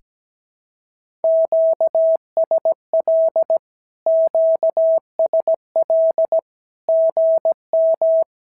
Morse code, heard as a keyed tone, about 17 wpm, 650 Hz. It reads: QSL QSL GM
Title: SOTA QSO Practice 17wpm